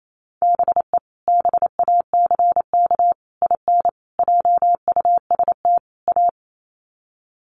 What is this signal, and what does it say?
Morse code, heard as a keyed tone, about 28 wpm, 700 Hz. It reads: BE BACK IN JUST A
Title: Sets of 5 Words Encore 28wpm